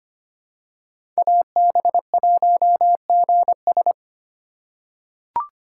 Morse code, heard as a keyed tone, about 25 wpm, 700 Hz. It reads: AB1GS
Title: USA Call Signs - 5 Characters - 2x Character Spacing - 25wpm